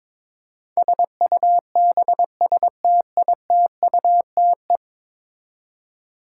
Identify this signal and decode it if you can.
Morse code, heard as a keyed tone, about 22 wpm, 700 Hz. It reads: SUBSTITUTE
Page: morse-code-ninja-practice-22wpm